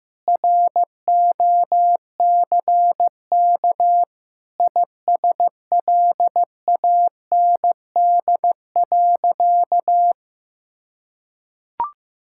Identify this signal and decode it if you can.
Morse code, heard as a keyed tone, about 15 wpm, 700 Hz. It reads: ROCK ISLAND.
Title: Sentences from Top 1600 Words 15wpm